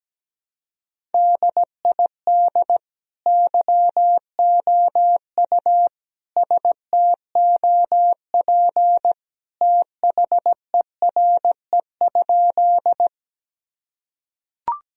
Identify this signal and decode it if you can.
Morse code, heard as a keyed tone, about 17 wpm, 700 Hz. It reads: DID YOU STOP THERE?
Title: Sentences from Top 1000 Words - Review Part 1 - 17wpm